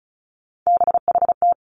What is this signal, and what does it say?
Morse code, heard as a keyed tone, about 36 wpm, 700 Hz. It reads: BHT
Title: Three Challenging Letter-Number Combinations 35wpm